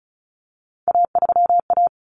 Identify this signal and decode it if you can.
Morse code, heard as a keyed tone, about 34 wpm, 700 Hz. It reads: A3A
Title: Common Cut Numbers - 3-Digit 35wpm